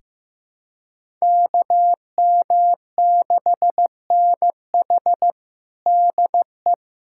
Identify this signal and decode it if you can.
Morse code, heard as a keyed tone, about 16 wpm, 700 Hz. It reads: KM6NH DE
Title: USA Callsign DE Callsign - 5 Characters - 15wpm